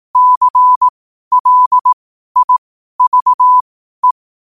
Morse code, heard as a keyed tone, about 18 wpm, spaced as for 12 wpm, 1000 Hz. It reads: CLIVE
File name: qotdfiles / 12wpm